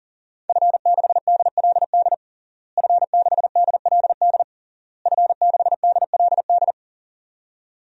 Morse code, heard as a keyed tone, about 37 wpm, 700 Hz. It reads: F6DLD F6DLD F6DLD
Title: DXCC-100 Call Signs - 5 Characters - Repeated 3x - 40wpm